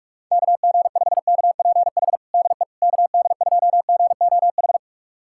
Morse code, heard as a keyed tone, about 46 wpm, 690 Hz. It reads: KG5KPH DE KD2GOH
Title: USA Callsign DE Callsign 45wpm